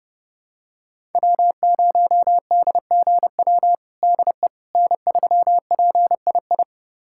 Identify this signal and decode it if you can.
Morse code, heard as a keyed tone, about 32 wpm, 700 Hz. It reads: W0DGW DE N3PII
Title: USA Callsign DE Callsign - 5 Characters - 30wpm